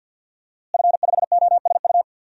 Morse code, heard as a keyed tone, about 53 wpm, 700 Hz. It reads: F5OSU
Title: DXCC-100 Call Signs - 5 Characters - Repeated 2x - 50wpm